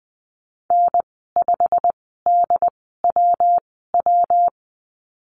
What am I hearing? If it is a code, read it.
Morse code, heard as a keyed tone, about 20 wpm, spaced as for 14 wpm, 700 Hz. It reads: N5DWW
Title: 140 Lesson 23g - Letter f - Call Signs 20wpm